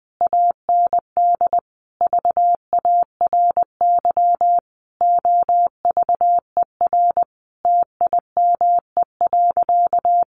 Morse code, heard as a keyed tone, about 20 wpm, 700 Hz. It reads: AND VARY OVER TIME.
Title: Sentences from Top 1200 Words 20wpm